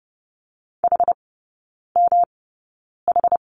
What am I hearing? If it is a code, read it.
Morse code, heard as a keyed tone, about 30 wpm, spaced as for 9 wpm, 700 Hz. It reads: HMH
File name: Lesson 18e - Letter m - Three Characters 30wpm